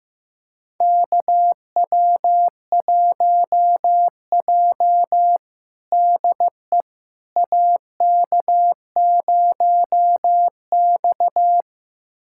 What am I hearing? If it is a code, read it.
Morse code, heard as a keyed tone, about 15 wpm, 700 Hz. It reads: KW1J DE AK0X
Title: USA Callsign DE Callsign - 4 Characters - 15wpm